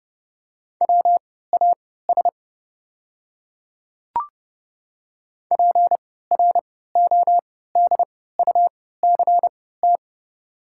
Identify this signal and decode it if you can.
Morse code, heard as a keyed tone, about 30 wpm, spaced as for 17 wpm, 700 Hz. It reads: WAS PRODUCT
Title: 3x Character Spacing - Top 500 Words - 30wpm